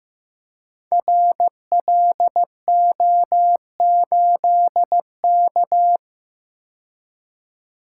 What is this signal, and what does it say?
Morse code, heard as a keyed tone, about 15 wpm, 700 Hz. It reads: RLO8K
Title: Code Groups - 5 Letter:Numbers - 15wpm